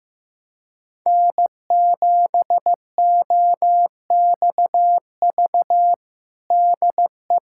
Morse code, heard as a keyed tone, about 16 wpm, 700 Hz. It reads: N7OXV DE
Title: USA Callsign DE Callsign - 5 Characters - 15wpm